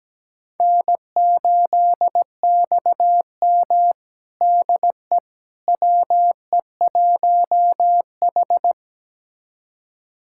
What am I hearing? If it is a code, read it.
Morse code, heard as a keyed tone, about 17 wpm, 700 Hz. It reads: N8XM DE WE1H
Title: USA Callsign DE Callsign - 4 Characters - 17wpm